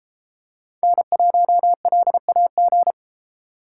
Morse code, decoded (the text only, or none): N1LAG